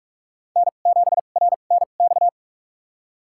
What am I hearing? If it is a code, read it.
Morse code, heard as a keyed tone, about 46 wpm, spaced as for 33 wpm, 690 Hz. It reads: N7RNX
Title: USA Call Signs - 5 Characters - 2x Character Spacing - 45wpm